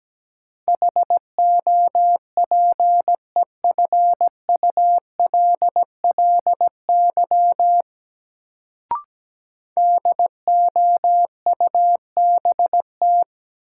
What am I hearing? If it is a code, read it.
Morse code, heard as a keyed tone, about 17 wpm, 700 Hz. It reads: HOPEFULLY DOUBT
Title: Sentences from Top 1500 Words - New Word Drill - 17wpm